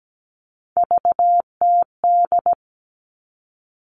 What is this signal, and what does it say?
Morse code, heard as a keyed tone, about 17 wpm, 700 Hz. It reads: VTD
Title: All Cut Numbers - 3-Digit 17wpm